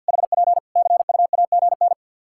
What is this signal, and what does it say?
Morse code, heard as a keyed tone, about 51 wpm, 700 Hz. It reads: HP CUAGN
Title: Practice QSO - Exchange 4 - Ending - 50wpm